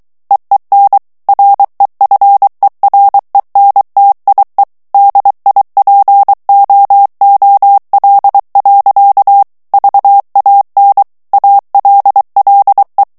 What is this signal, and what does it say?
Morse code, heard as a keyed tone, about 23 wpm, 800 Hz. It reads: EEN REFERENTIE DIPOOL. VAN ALLE
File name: HOLLANDAIS texte clair 1 vitesse 22 mots minute